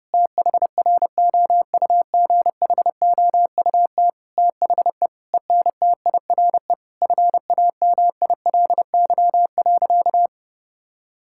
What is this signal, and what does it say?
Morse code, heard as a keyed tone, about 30 wpm, 700 Hz. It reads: THROUGHOUT THE ENTIRE FAMILY.
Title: Sentences from Top 1000 Words - Review Part 5 - 30wpm